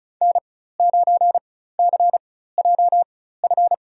Morse code, heard as a keyed tone, about 34 wpm, spaced as for 16 wpm, 700 Hz. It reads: N9CJF
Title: USA Call Signs - 5 Characters - 4x Character Spacing - 35wpm